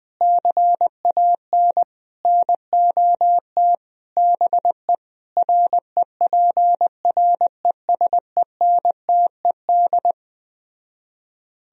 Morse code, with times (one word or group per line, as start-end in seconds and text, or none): 0.2-1.8 CAN
2.2-3.7 NOT
4.2-4.9 BE
5.4-10.1 REPRESENTED